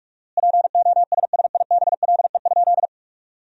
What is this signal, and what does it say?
Morse code, heard as a keyed tone, about 46 wpm, 700 Hz. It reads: POSSIBLE?
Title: 4x Word-Spacing - Sentences from Top 500 Words - 45wpm